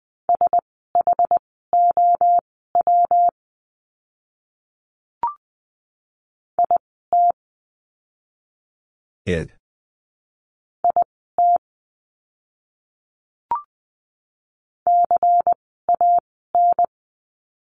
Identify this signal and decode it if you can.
Morse code, heard as a keyed tone, about 20 wpm, spaced as for 15 wpm, 700 Hz. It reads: SHOW IT IT CAN
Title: 230 Lesson 37f - Letter x - Words 20wpm